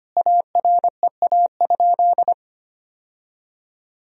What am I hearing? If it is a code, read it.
Morse code, heard as a keyed tone, about 25 wpm, 700 Hz. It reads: AREA?
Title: Sentences from Top 1000 Words - Review Part 2 - 25wpm